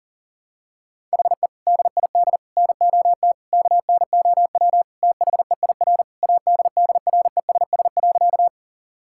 Morse code, heard as a keyed tone, about 40 wpm, 700 Hz. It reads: HE DID NOT KNOW THEIR ADDRESS.
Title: Sentences from Top 1000 Words - Review Part 6 - 40wpm